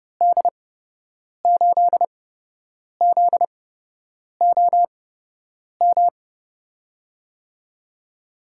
Morse code, heard as a keyed tone, about 31 wpm, spaced as for 8 wpm, 700 Hz. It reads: D8ZOM
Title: USA Call Signs - 6 Characters - 8x Character Spacing - 30wpm